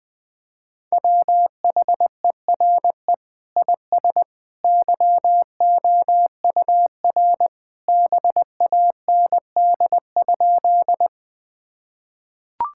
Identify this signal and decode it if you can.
Morse code, heard as a keyed tone, about 20 wpm, 700 Hz. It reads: WHERE IS YOUR BAND?
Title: Sentences from Top 1600 Words 20wpm